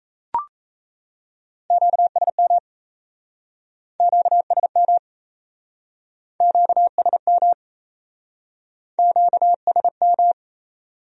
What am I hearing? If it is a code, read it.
Morse code, heard as a keyed tone, about 35 wpm, 700 Hz. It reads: QSM QSM QSM QSM
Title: Speed-Racing Q-Codes - 28wpm